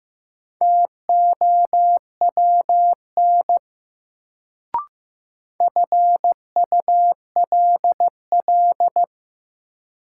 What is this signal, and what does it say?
Morse code, heard as a keyed tone, about 15 wpm, 700 Hz. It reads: TOWN FULL
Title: Sentences from Top 500 Words - New Word Drill - 15wpm